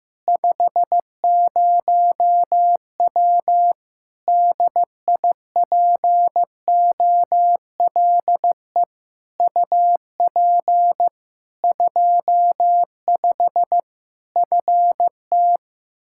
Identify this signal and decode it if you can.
Morse code, heard as a keyed tone, about 15 wpm, 700 Hz. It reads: 50W DIPOLE UP 25 FT
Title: Practice QSO - Exchange 2 - Info - 15wpm